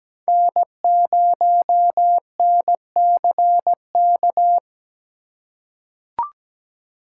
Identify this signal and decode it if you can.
Morse code, heard as a keyed tone, about 17 wpm, 700 Hz. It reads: N0NCK
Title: USA Call Signs - 8x Character Spacing - 17wpm